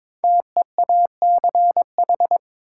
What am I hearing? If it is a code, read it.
Morse code, heard as a keyed tone, about 23 wpm, 700 Hz. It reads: TEACH
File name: 5 Letter Words - First 3 Letters Given - 22wpm